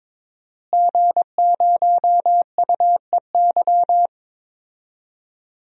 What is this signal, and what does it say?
Morse code, heard as a keyed tone, about 23 wpm, 700 Hz. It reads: G0UEY
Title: DXCC-100 Call Signs - 5 Characters - Repeated 2x - 22wpm